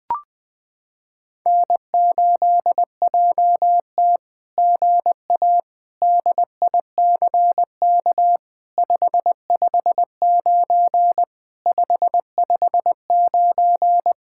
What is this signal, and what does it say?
Morse code, heard as a keyed tone, about 20 wpm, 700 Hz. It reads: N8JT GA DICK 559 559
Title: SOTA QSO Practice 20wpm